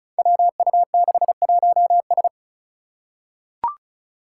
Morse code, heard as a keyed tone, about 36 wpm, 700 Hz. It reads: WU61S
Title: Five Challenging Letter-Number Combinations 35wpm